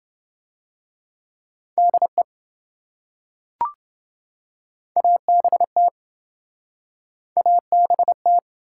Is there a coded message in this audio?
Yes: Morse code from 1.8 to 8.4 s, heard as a keyed tone, about 27 wpm, 700 Hz. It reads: DE ABT ABT